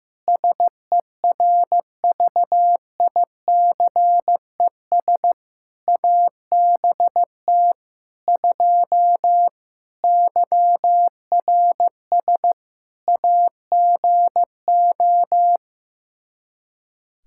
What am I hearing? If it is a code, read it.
Morse code, heard as a keyed tone, about 15 wpm, 700 Hz. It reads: SERVICES ABT 2 YRS AGO
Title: Practice QSO - Exchange 3 - More Info - 15wpm